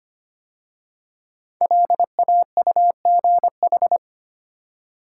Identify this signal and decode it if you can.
Morse code, heard as a keyed tone, about 25 wpm, 700 Hz. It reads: LAUGH
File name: 5 Letter Words - First 3 Letters Given - 25wpm